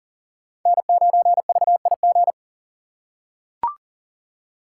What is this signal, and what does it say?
Morse code, heard as a keyed tone, about 40 wpm, 700 Hz. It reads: N9VIG
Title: USA Call Signs - 4x Character Spacing - 40wpm